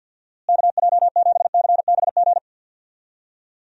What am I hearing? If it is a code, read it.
Morse code, heard as a keyed tone, about 50 wpm, 700 Hz. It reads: KJ7CBG
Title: USA Call Signs - 6 Characters - Repeated 2x - 50wpm